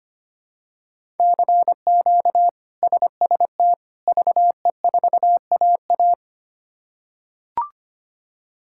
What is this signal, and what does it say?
Morse code, heard as a keyed tone, about 25 wpm, 700 Hz. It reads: CQ SST VE4AA
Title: K1USN SST QSO Practice - 25wpm